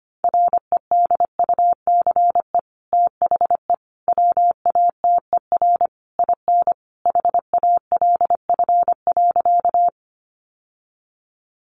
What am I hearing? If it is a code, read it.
Morse code, heard as a keyed tone, about 25 wpm, 700 Hz. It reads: REDUCE THE WATER IN HALF.
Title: Sentences from Top 1100 Words 25wpm